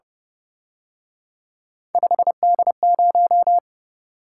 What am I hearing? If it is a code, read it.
Morse code, heard as a keyed tone, about 30 wpm, 700 Hz. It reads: 5D0
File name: Speed-Racing - Three Letter-Number Combinations - 25wpm